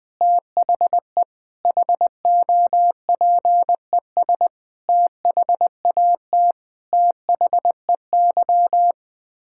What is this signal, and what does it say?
Morse code, heard as a keyed tone, about 20 wpm, 700 Hz. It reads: THE HOPES THAT THEY
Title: Sets of 9 Words - 20wpm